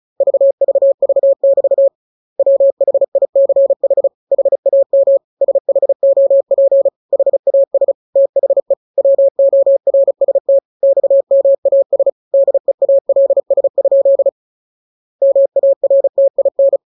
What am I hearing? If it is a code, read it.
Morse code, heard as a keyed tone, about 34 wpm, 550 Hz. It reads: VVV<BT> WHICH HAM SHOP HAS THE WORST XMAS DEALS? MARTIN
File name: jokes / joke 7 35 0000